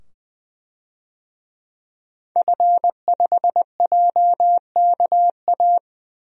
Morse code, heard as a keyed tone, about 20 wpm, 700 Hz. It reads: F5JKA
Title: DXCC-100 Call Signs - 5 Characters - 20wpm